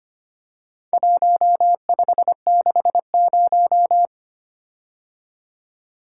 Morse code, heard as a keyed tone, about 25 wpm, 700 Hz. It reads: 1560
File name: practice / 25wpm / Four Letter-Number Combinations - 25wpm